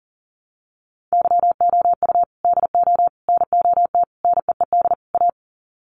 Morse code, heard as a keyed tone, about 40 wpm, 700 Hz. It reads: YOU DO NOT NEED A
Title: Sets of 5 Words Encore 40wpm